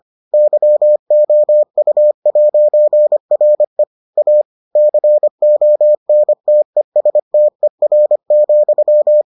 Morse code, heard as a keyed tone, about 25 wpm, 600 Hz. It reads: YOU'RE A CONTESTER,